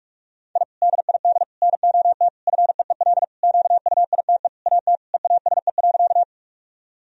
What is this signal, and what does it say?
Morse code, heard as a keyed tone, about 46 wpm, 700 Hz. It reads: I DID NOT FEEL QUITE AT EASE.
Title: Sentences from Top 1000 Words - Review Part 1 - 45wpm